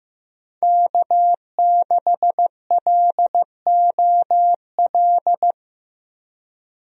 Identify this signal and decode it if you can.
Morse code, heard as a keyed tone, about 16 wpm, 700 Hz. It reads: K6LOL
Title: USA Call Signs - 5 Characters - 15wpm